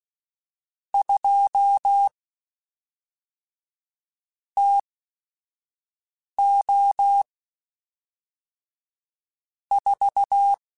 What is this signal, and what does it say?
Morse code, heard as a keyed tone, about 16 wpm, spaced as for 5 wpm, 800 Hz. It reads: 2 TO 4
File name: Weather Sets / Weather Sets-mp3s / Weather 4 14 @5